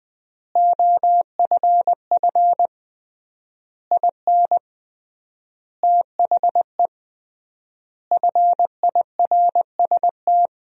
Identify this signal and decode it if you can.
Morse code, heard as a keyed tone, about 20 wpm, 700 Hz. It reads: OFF IN THE FIRST
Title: 3x Word-Spacing - Sentences from Top 200 Words - 20wpm